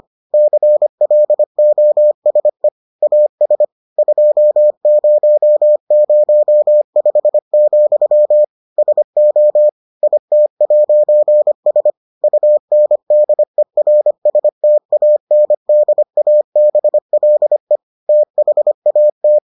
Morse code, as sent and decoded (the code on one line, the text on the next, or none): -.-. .-.. --- ... / .- ... / ..--- ----- ----- ..... --..-- / ... --- / .. - .----. ... / ..- -. -.. .-. ... - .- -. -.. .- -... .-.. / - .... .- -
CLOSE AS 2005, SO IT'S UNDERSTANDABLE THAT